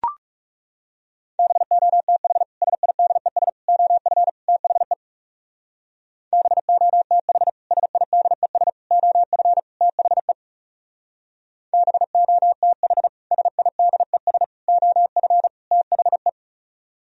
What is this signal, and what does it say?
Morse code, heard as a keyed tone, about 37 wpm, 700 Hz. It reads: BOTH SIDES OF THE BOTH SIDES OF THE BOTH SIDES OF THE
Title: Speed-Racing - Sets of 4 Words 30wpm